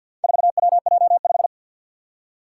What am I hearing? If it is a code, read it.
Morse code, heard as a keyed tone, about 47 wpm, 700 Hz. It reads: 4WJ5
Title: Four Challenging Letter-Number Combinations 50wpm